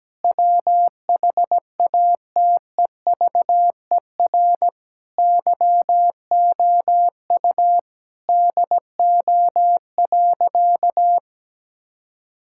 Morse code, heard as a keyed tone, about 17 wpm, 700 Hz. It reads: WHATEVER YOU DO.
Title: Sentences from Top 1600 Words 17wpm